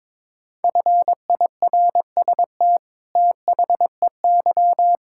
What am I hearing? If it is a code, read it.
Morse code, heard as a keyed tone, about 22 wpm, 700 Hz. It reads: FIRST THEY
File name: practice / 22wpm / Sets of 8 Words - 22wpm